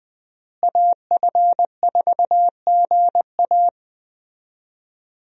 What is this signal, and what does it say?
Morse code, heard as a keyed tone, about 20 wpm, 700 Hz. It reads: AF4GA
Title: USA Call Signs - 5 Characters - Repeated 2x - 20wpm